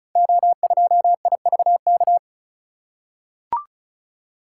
Morse code, heard as a keyed tone, about 36 wpm, 690 Hz. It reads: O2IVK
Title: Code Groups - 5 Letter:Numbers - 35wpm